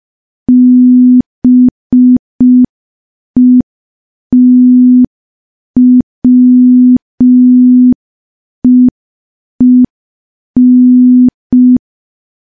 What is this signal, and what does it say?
Morse code, heard as a keyed tone, about 5 wpm, 250 Hz. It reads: BETWEEN
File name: morse / morse-training-05WPM